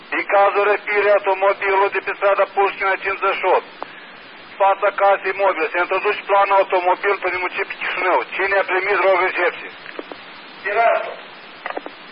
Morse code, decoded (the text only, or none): none